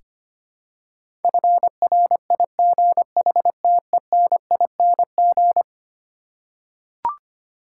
Morse code, heard as a keyed tone, about 25 wpm, 700 Hz. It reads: FRIGHTENING
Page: morse-code-ninja-practice-25wpm